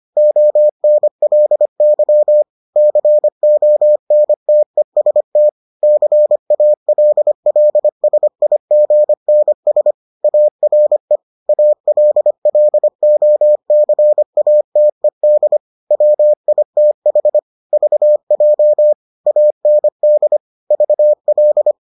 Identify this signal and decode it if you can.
Morse code, heard as a keyed tone, about 25 wpm, 600 Hz. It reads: ONLY CONTEST CALLSIGNS ARE ALLOCATED WITH VJ AND VL